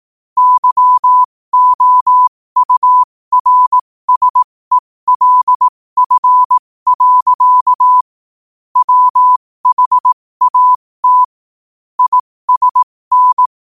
Morse code, heard as a keyed tone, about 18 wpm, spaced as for 15 wpm, 1000 Hz. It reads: YOURSELF. WHAT ISN
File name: qotdfiles / 15wpm